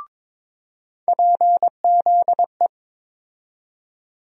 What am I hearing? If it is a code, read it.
Morse code, heard as a keyed tone, about 21 wpm, 700 Hz. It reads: PZE
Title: USA Call Sign Postfixes - 3 Letters 22wpm